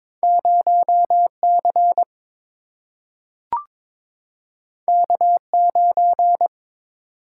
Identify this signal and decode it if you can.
Morse code, heard as a keyed tone, about 22 wpm, 700 Hz. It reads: 0 C K 9
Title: Single-Letter Single-Number 22wpm